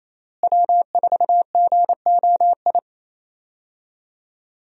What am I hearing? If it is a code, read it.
Morse code, heard as a keyed tone, about 28 wpm, 700 Hz. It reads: W4GOI
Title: USA Call Signs - 5 Characters - Repeated 3x - 28wpm